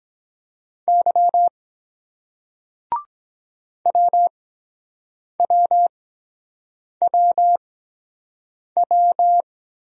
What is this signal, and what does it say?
Morse code, heard as a keyed tone, about 23 wpm, spaced as for 7 wpm, 700 Hz. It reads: Y WWWW